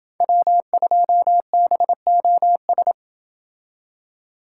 Morse code, heard as a keyed tone, about 28 wpm, 700 Hz. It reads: W2BOS